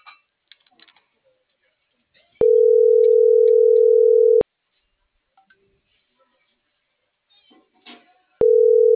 Ringback tone (North America), 440 + 480 Hz.